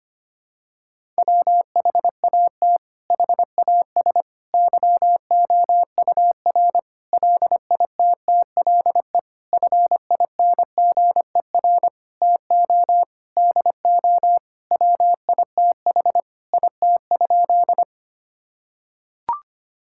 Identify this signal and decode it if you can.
Morse code, heard as a keyed tone, about 25 wpm, 700 Hz. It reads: WHAT HAS YOUR LITTLE FINGER TO DO WITH IT?
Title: Sentences from Top 1600 Words 25wpm